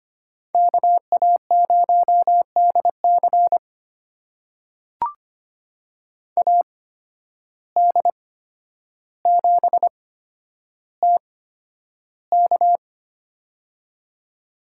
Morse code, heard as a keyed tone, about 25 wpm, spaced as for 11 wpm, 700 Hz. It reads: KA0DC AD7TK
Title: USA Call Signs - 5 Characters - 8x Character Spacing - 25wpm